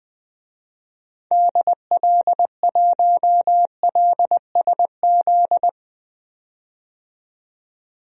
Morse code, heard as a keyed tone, about 20 wpm, 700 Hz. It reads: DL1LSZ
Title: DXCC-100 Call Signs - 6 Characters - 20wpm